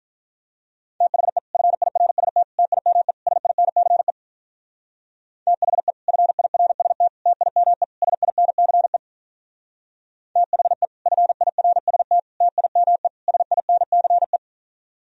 Morse code, heard as a keyed tone, about 43 wpm, 700 Hz. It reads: THE FIRST TIME SINCE THE FIRST TIME SINCE THE FIRST TIME SINCE
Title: Speed-Racing - Sets of 4 Words 35wpm